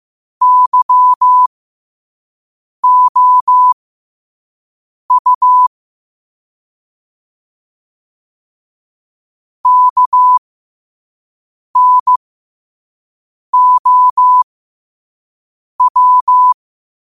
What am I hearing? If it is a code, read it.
Morse code, heard as a keyed tone, about 15 wpm, spaced as for 5 wpm, 1000 Hz. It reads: YOU KNOW
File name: 5wpm